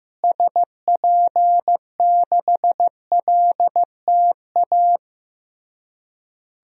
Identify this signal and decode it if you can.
Morse code, heard as a keyed tone, about 15 wpm, 700 Hz. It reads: SP6LTA
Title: DXCC-100 Call Signs - 6 Characters - Repeated 3x - 15wpm